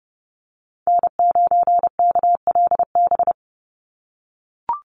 Morse code, heard as a keyed tone, about 30 wpm, 700 Hz. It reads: N9KLB
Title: USA Call Signs 30wpm